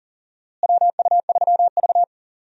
Morse code, heard as a keyed tone, about 40 wpm, 700 Hz. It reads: WU3V
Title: USA Call Signs - 4 Characters - Repeated 3x - 40wpm